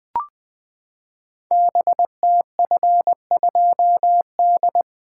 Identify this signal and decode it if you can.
Morse code, heard as a keyed tone, about 20 wpm, 700 Hz. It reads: BTF2D
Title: Code Groups - 5 Letter:Numbers - 20wpm